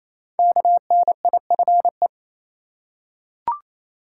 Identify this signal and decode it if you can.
Morse code, heard as a keyed tone, about 27 wpm, 700 Hz. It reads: KNIFE
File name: Easy Words - Speed-Racing with Hints - 22wpm